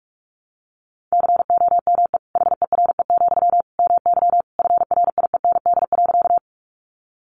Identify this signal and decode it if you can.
Morse code, heard as a keyed tone, about 44 wpm, 700 Hz. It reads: COME HERE, MY FRIEND.
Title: Sentences from Top 300 Words 45wpm